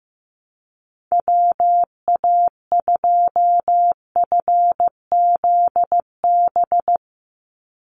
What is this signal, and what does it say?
Morse code, heard as a keyed tone, about 15 wpm, 700 Hz. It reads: WA2FZB
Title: Call Signs Repeated 2x - 15wpm